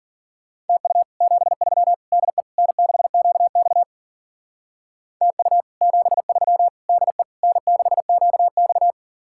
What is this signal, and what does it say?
Morse code, heard as a keyed tone, about 45 wpm, 700 Hz. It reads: TU 73 DE N6QX TU 73 DE N6QX
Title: Speed-Racing Common QSO Phrases - 40wpm